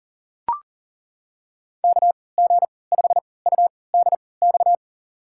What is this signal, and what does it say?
Morse code, heard as a keyed tone, about 40 wpm, spaced as for 23 wpm, 700 Hz. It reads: KG5UDX
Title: USA Call Signs - 6 Characters - 3x Character Spacing - 40wpm